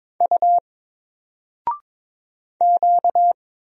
Morse code, heard as a keyed tone, about 22 wpm, 700 Hz. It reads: U Q
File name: Single-Letter-Number-22wpm